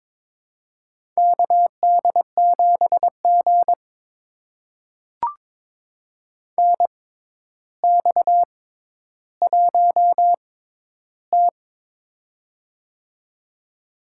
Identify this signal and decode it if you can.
Morse code, heard as a keyed tone, about 22 wpm, spaced as for 11 wpm, 700 Hz. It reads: KD7G NX1T
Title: USA Call Signs - 4 Characters - 6x Character Spacing - 22wpm